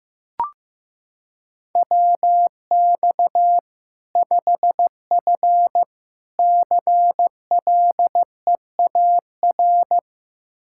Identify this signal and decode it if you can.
Morse code, heard as a keyed tone, about 15 wpm, 700 Hz. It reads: WX 5F CLEAR
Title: Practice QSO - Exchange 2 - Info - 15wpm